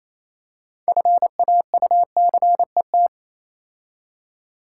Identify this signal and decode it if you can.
Morse code, heard as a keyed tone, about 27 wpm, 700 Hz. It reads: FAUCET
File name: Easy Words - Speed-Racing with Hints - 22wpm